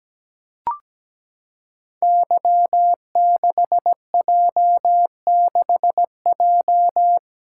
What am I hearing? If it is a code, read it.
Morse code, heard as a keyed tone, about 17 wpm, 700 Hz. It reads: Y6J6J
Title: Code Groups - 5 Letter:Numbers - 17wpm